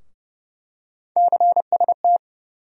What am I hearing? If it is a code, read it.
Morse code, heard as a keyed tone, about 30 wpm, 700 Hz. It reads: CST
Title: USA Call Sign Postfixes - 3 Letters 30wpm